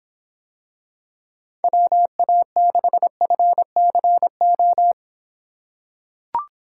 Morse code, heard as a keyed tone, about 26 wpm, 700 Hz. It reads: WA6FCO